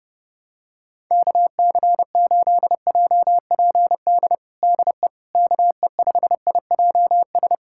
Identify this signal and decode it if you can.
Morse code, heard as a keyed tone, about 30 wpm, 700 Hz. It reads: KC8JPD DE KE5IJS
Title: USA Callsign DE Callsign - 6 Characters - 30wpm